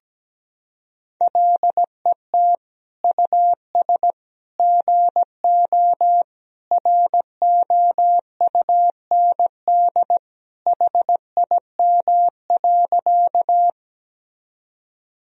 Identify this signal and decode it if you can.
Morse code, heard as a keyed tone, about 17 wpm, 700 Hz. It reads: LET US GO ROUND HIM.
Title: Sentences from Top 1000 Words - Review Part 1 - 17wpm